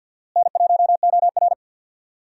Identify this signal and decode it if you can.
Morse code, heard as a keyed tone, about 48 wpm, 700 Hz. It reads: N1OR
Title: USA Call Signs - 4 Characters - Repeated 2x - 50wpm